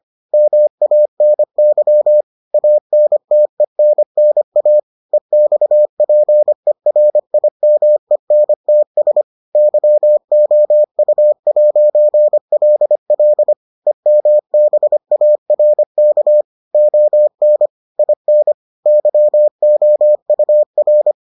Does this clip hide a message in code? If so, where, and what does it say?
Yes: Morse code from 0.3 to 21.1 s, heard as a keyed tone, about 25 wpm, 600 Hz. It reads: MANY ANTENNA EXPERIMENTS YOU'LL EMBARK ON IN YOUR